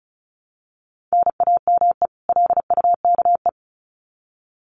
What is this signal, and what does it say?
Morse code, heard as a keyed tone, about 34 wpm, 700 Hz. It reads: NAME LUKE
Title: Common QSO Phrases 35wpm